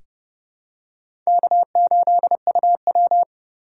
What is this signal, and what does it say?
Morse code, heard as a keyed tone, about 30 wpm, 700 Hz. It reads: K8UW
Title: USA Call Signs - 3x Character Spacing - 30wpm